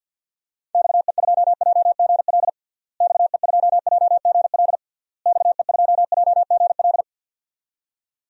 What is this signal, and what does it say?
Morse code, heard as a keyed tone, about 46 wpm, 700 Hz. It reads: XE2JGL XE2JGL XE2JGL
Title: DXCC-100 Call Signs - 6 Characters - Repeated 3x - 50wpm